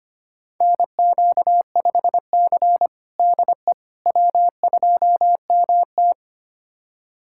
Morse code, heard as a keyed tone, about 25 wpm, 700 Hz. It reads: NQ5C DE W2MT
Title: USA Callsign DE Callsign - 4 Characters - 25wpm